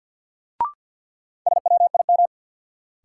Morse code, heard as a keyed tone, about 53 wpm, 700 Hz. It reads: SWIM
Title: Sentences from Top 1000 Words - Review Part 6 - New Word Drill - 50wpm